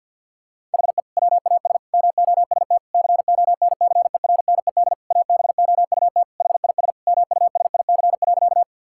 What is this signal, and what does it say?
Morse code, heard as a keyed tone, about 50 wpm, 700 Hz. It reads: HE WAS MOST CONCERNED ABOUT HIS MUSIC.
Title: Sentences from Top 1000 Words - Review Part 6 - 50wpm